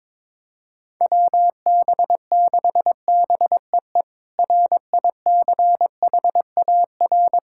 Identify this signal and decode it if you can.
Morse code, heard as a keyed tone, about 21 wpm, 700 Hz. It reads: WB6BEE RICHAR
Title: K1USN SST QSO Practice - 22wpm